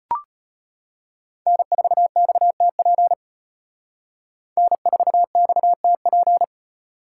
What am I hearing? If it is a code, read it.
Morse code, heard as a keyed tone, about 33 wpm, 700 Hz. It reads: N4XTP N4XTP